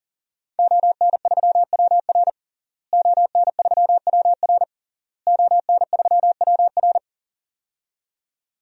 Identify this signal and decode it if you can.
Morse code, heard as a keyed tone, about 37 wpm, 700 Hz. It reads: ON3WR ON3WR ON3WR
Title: DXCC-100 Call Signs - 5 Characters - Repeated 3x - 40wpm